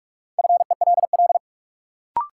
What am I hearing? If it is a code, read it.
Morse code, heard as a keyed tone, about 46 wpm, 700 Hz. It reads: FELL
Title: Sentences from Top 1600 Words - New Word Drill Alt - 45wpm